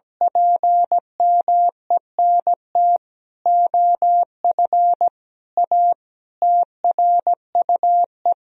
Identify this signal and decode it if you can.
Morse code, heard as a keyed tone, about 17 wpm, 700 Hz. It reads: PMENT OF A TRUE
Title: Sentences from Top 1000 Words - Review Part 4 - 17wpm